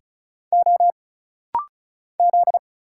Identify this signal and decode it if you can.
Morse code, heard as a keyed tone, about 34 wpm, 700 Hz. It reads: O Z
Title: Single Letters - Rapid-Fire 35wpm